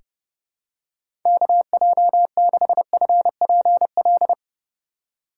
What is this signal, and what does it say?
Morse code, heard as a keyed tone, about 31 wpm, 700 Hz. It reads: KJ6FPL
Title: USA Call Signs - 6 Characters - 3x Character Spacing - 30wpm